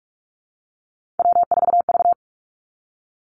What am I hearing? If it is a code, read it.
Morse code, heard as a keyed tone, about 46 wpm, 700 Hz. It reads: W4V
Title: Three Challenging Letter-Number Combinations 45wpm